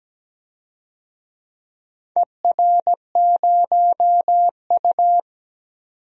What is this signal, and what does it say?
Morse code, heard as a keyed tone, about 17 wpm, 700 Hz. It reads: ER0U
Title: DXCC-100 Call Signs - 4 Characters - 17wpm